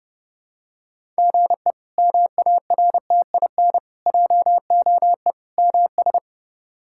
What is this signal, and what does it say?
Morse code, heard as a keyed tone, about 30 wpm, 700 Hz. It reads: GE MARTIN JOE MS
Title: K1USN SST QSO Practice - 30wpm